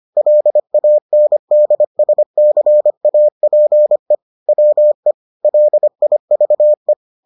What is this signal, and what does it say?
Morse code, heard as a keyed tone, about 25 wpm, 600 Hz. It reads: LANDSCAPE WE LIVE